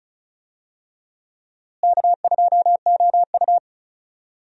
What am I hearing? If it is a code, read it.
Morse code, heard as a keyed tone, about 36 wpm, 700 Hz. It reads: K2OU
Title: USA Call Signs - 4 Characters - Repeated 3x - 35wpm